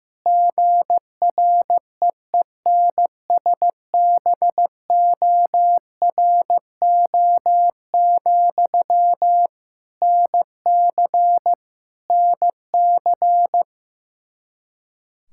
Morse code, heard as a keyed tone, about 16 wpm, 700 Hz. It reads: GREENSBORO, NC NC
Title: Practice QSO - Exchange 1 - Beginning - 15wpm